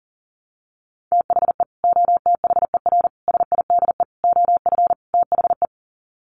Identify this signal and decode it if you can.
Morse code, heard as a keyed tone, about 40 wpm, 700 Hz. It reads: THE OTHER SIDE OF THE
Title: Sets of 5 Words 40wpm